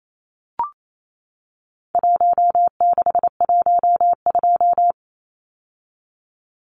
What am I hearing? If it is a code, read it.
Morse code, heard as a keyed tone, about 28 wpm, 700 Hz. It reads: 1612
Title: Numbers - Contest Style - 4-Digit 28wpm